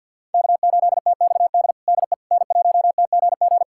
Morse code, heard as a keyed tone, about 51 wpm, 700 Hz. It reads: K8TXD DE N1TGG
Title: USA Callsign DE Callsign - 5 Characters - 50wpm